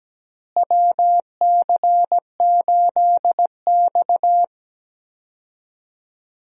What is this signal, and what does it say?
Morse code, heard as a keyed tone, about 17 wpm, 700 Hz. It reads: WC8X
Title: USA Call Signs - 4 Characters - Repeated 3x - 17wpm